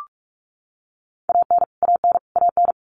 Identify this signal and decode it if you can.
Morse code, heard as a keyed tone, about 44 wpm, 700 Hz. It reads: AN AN AN
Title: Top 200 Words Repeated 45wpm